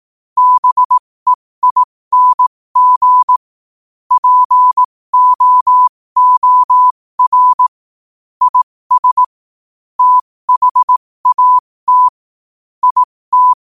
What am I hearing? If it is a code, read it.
Morse code, heard as a keyed tone, about 18 wpm, spaced as for 15 wpm, 1000 Hz. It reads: BEING POOR IS THAT IT
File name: qotdfiles / 15wpm